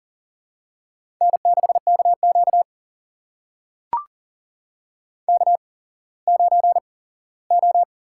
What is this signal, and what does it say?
Morse code, heard as a keyed tone, about 41 wpm, spaced as for 19 wpm, 700 Hz. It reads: N6KQ K9O
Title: USA Call Signs - 4 Characters - 8x Character Spacing - 40wpm